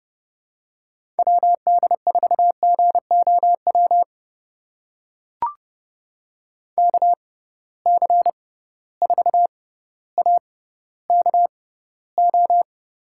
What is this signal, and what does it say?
Morse code, heard as a keyed tone, about 31 wpm, spaced as for 16 wpm, 700 Hz. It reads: WD4GOW KC4AKO
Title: USA Call Signs - 6 Characters - 6x Character Spacing - 30wpm